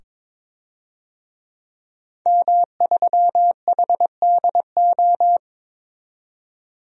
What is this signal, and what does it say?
Morse code, heard as a keyed tone, about 22 wpm, 700 Hz. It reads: M3HDO
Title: DXCC-100 Call Signs - 5 Characters - 22wpm